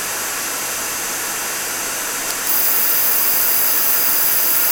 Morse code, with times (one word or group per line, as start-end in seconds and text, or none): none